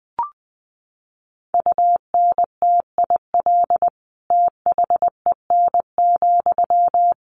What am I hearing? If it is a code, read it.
Morse code, heard as a keyed tone, about 20 wpm, 700 Hz. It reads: UNTIL THEN,
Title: Sentences from Top 1500 Words 20wpm